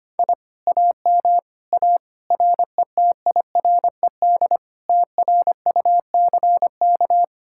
Morse code, heard as a keyed tone, about 31 wpm, 700 Hz. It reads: I AM A RETIRED TRUCK
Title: Speed-Racing Common QSO Phrases - 25wpm